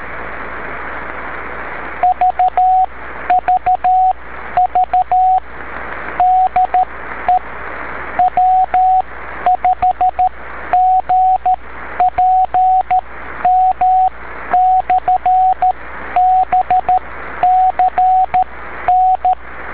Morse code, heard as a keyed tone, about 13 wpm, spaced as for 11 wpm, 740 Hz, with radio noise. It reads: VVV DE W5GPM/BCN
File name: beacons / w5gpm-b